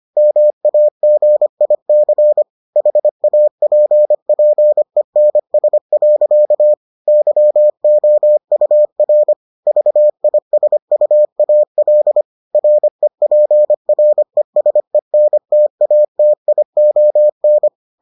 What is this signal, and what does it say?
Morse code, heard as a keyed tone, about 25 wpm, 600 Hz. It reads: MAGIC HAPPENS. YOUR VISUAL REPRESENTATION